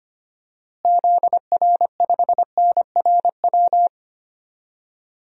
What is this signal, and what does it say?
Morse code, heard as a keyed tone, about 25 wpm, 700 Hz. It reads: ZR5NRW
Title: DXCC-100 Call Signs - 6 Characters - Repeated 3x - 25wpm